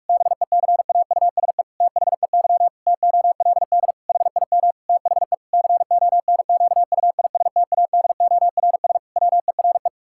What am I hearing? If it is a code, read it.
Morse code, heard as a keyed tone, about 46 wpm, 700 Hz. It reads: BECAUSE THEY TOLD HIM THE CONQUISTADORS WERE
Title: Pangram Challenge v2 - 45wpm